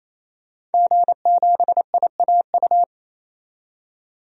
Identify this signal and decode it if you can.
Morse code, heard as a keyed tone, about 27 wpm, 700 Hz. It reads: G7IAU
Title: DXCC-100 Call Signs - 5 Characters - 28wpm